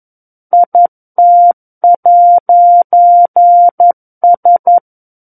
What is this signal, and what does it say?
Morse code, heard as a keyed tone, about 11 wpm, 700 Hz. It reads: IT'S
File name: 11 WPM